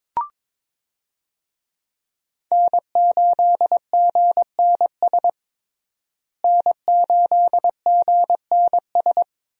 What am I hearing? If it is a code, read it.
Morse code, heard as a keyed tone, about 22 wpm, 700 Hz. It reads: N8GNS N8GNS
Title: USA Call Signs - 5 Characters - Repeated 2x - 22wpm